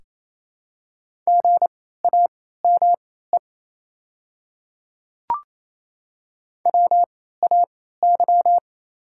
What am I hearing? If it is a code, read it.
Morse code, heard as a keyed tone, about 28 wpm, spaced as for 16 wpm, 700 Hz. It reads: GAME WAY
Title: 3x Character Spacing - Top 500 Words - 28wpm